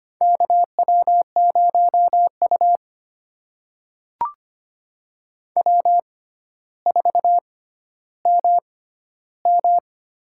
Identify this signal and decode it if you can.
Morse code, heard as a keyed tone, about 25 wpm, spaced as for 13 wpm, 700 Hz. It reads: KW0U W4MM
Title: USA Call Signs - 4 Characters - 6x Character Spacing - 25wpm